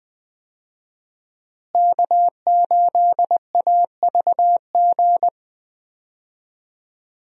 Morse code, heard as a keyed tone, about 20 wpm, 700 Hz. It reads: K8AVG
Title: USA Call Signs - 5 Characters - Repeated 3x - 20wpm